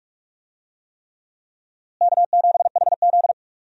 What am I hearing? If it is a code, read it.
Morse code, heard as a keyed tone, about 46 wpm, 700 Hz. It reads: K7HZ
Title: USA Call Signs - 4 Characters - Repeated 3x - 45wpm